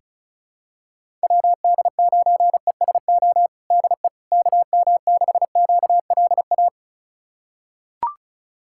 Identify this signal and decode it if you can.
Morse code, heard as a keyed tone, about 36 wpm, 700 Hz. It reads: WD9ESO DE KM6QLA
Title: USA Callsign DE Callsign 35wpm